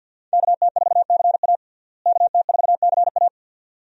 Morse code, heard as a keyed tone, about 46 wpm, 700 Hz. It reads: KT4CA KT4CA
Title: USA Call Signs - 5 Characters - Repeated 2x - 50wpm